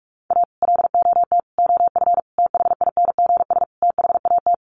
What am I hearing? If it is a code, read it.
Morse code, heard as a keyed tone, about 44 wpm, 700 Hz. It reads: A LOT OF THINGS THAT
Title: Sets of 5 Words 45wpm